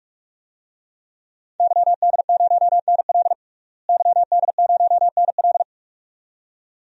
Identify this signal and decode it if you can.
Morse code, heard as a keyed tone, about 46 wpm, 700 Hz. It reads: YD0NL YD0NL
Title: DXCC-100 Call Signs - 5 Characters - Repeated 2x - 45wpm